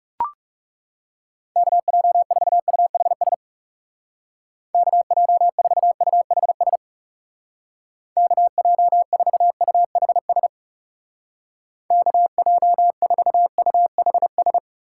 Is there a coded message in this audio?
Yes: Morse code from 1.6 to 14.6 s, heard as a keyed tone, about 38 wpm, 700 Hz. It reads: KJ4UHS KJ4UHS KJ4UHS KJ4UHS